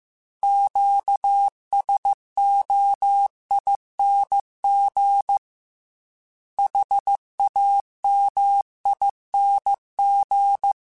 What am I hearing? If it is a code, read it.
Morse code, heard as a keyed tone, about 15 wpm, 800 Hz. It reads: QSOING HAMING
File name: Special INGs-mp3s / Special INGs-mp3s / Sngl ing @13